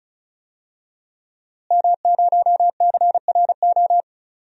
Morse code, heard as a keyed tone, about 36 wpm, 700 Hz. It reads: M0CRO
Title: DXCC-100 Call Signs - 5 Characters - Repeated 2x - 35wpm